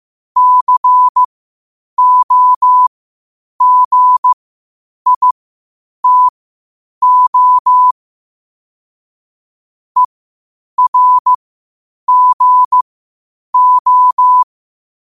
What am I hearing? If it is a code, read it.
Morse code, heard as a keyed tone, about 15 wpm, spaced as for 8 wpm, 1000 Hz. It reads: COGITO ERGO
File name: qotdfiles / 8wpm